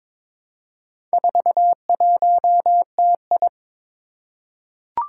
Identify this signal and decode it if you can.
Morse code, heard as a keyed tone, about 22 wpm, 700 Hz. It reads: 41TI
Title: Four Challenging Letter-Number Combinations 22wpm